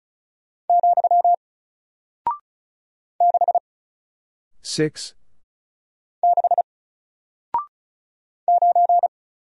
Morse code, heard as a keyed tone, about 36 wpm, 700 Hz. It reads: , 6 6 9
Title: Single Character (Letter:Number:Punctuation:Prosign) Alt 35wpm